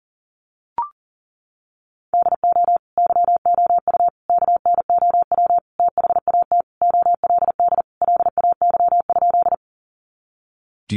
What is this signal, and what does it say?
Morse code, heard as a keyed tone, about 40 wpm, 700 Hz. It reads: DO YOU KNOW THAT OLD LAY?
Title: Sentences from Top 400 Words 40wpm